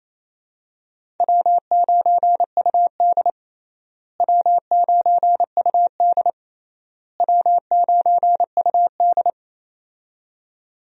Morse code, heard as a keyed tone, about 28 wpm, 700 Hz. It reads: W9UD W9UD W9UD
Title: USA Call Signs - 4 Characters - Repeated 3x - 28wpm